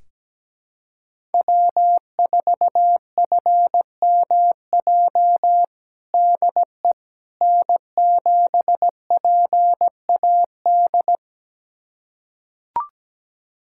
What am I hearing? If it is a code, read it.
Morse code, heard as a keyed tone, about 17 wpm, 700 Hz. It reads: W4FMJ DE N7PAD
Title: USA Callsign DE Callsign - 5 Characters - 17wpm